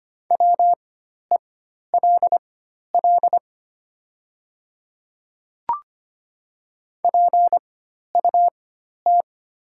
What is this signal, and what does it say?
Morse code, heard as a keyed tone, about 25 wpm, spaced as for 12 wpm, 700 Hz. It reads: WELL PUT
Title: 4x Character Spacing - Top 200 Words - 25wpm